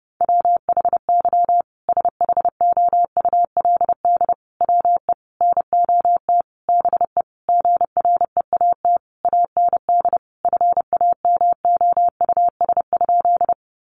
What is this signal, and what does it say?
Morse code, heard as a keyed tone, about 30 wpm, 700 Hz. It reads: WHY SHOULD WE NOT BE GREAT AND FAMOUS?
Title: Sentences from Top 1500 Words 30wpm